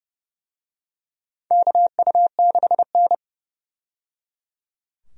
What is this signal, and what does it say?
Morse code, heard as a keyed tone, about 30 wpm, 700 Hz. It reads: KU6N
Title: USA Call Signs - 4 Characters - 30wpm